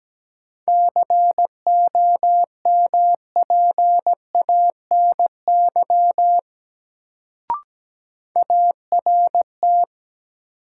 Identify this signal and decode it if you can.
Morse code, heard as a keyed tone, about 17 wpm, 700 Hz. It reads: COMPANY ART
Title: Sentences from Top 600 Words - New Word Drill - 17wpm